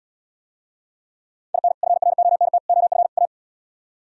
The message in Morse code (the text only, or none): IN FRONT OF A